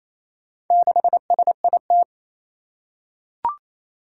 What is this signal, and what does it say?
Morse code, heard as a keyed tone, about 29 wpm, 700 Hz. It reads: 6SIT
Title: Four Challenging Letter-Number Combinations 28wpm